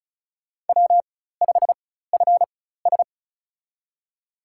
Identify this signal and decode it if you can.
Morse code, heard as a keyed tone, about 36 wpm, spaced as for 16 wpm, 700 Hz. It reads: W5FS
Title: USA Call Signs - 4x Character Spacing - 35wpm